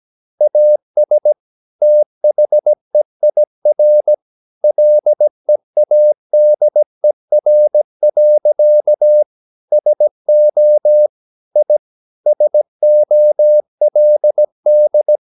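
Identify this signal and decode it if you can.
Morse code, heard as a keyed tone, about 17 wpm, 600 Hz. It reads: AS THEIR LEADER. SO I SOLD
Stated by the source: 17 wpm Morse practice sending the English text of a novel.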